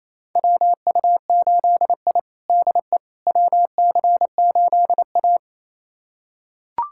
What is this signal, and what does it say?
Morse code, heard as a keyed tone, about 28 wpm, 700 Hz. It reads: WU8I DE WC8A
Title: USA Callsign DE Callsign - 4 Characters - 28wpm